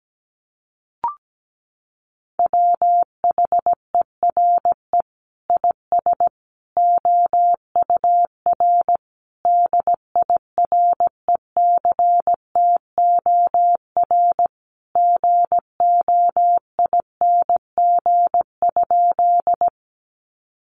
Morse code, heard as a keyed tone, about 17 wpm, 700 Hz. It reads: WHERE IS OUR DIRECTOR GOING?
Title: Sentences from Top 1300 Words 17wpm